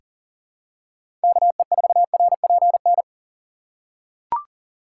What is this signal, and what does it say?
Morse code, heard as a keyed tone, about 40 wpm, 700 Hz. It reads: KE4RPN
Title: USA Call Signs - 3x Character Spacing - 40wpm